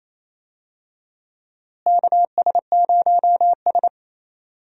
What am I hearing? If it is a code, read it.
Morse code, heard as a keyed tone, about 28 wpm, 700 Hz. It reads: KS0S
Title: USA Call Signs - 4 Characters - Repeated 2x - 28wpm